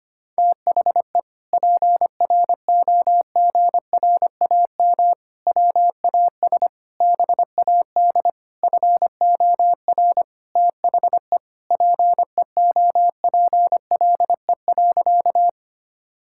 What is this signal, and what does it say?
Morse code, heard as a keyed tone, about 25 wpm, 700 Hz. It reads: THE PROGRAM WAS BAD FOR THE PEOPLE.
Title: Sentences from Top 1000 Words - Review Part 5 - 25wpm